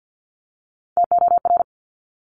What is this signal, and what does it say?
Morse code, heard as a keyed tone, about 50 wpm, 700 Hz. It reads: TOR
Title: Top Three Letter Combinations in Words 50wpm